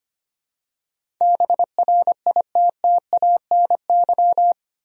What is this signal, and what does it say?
Morse code, heard as a keyed tone, about 25 wpm, 700 Hz. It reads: BRITTANY
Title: USA Common Names - 25wpm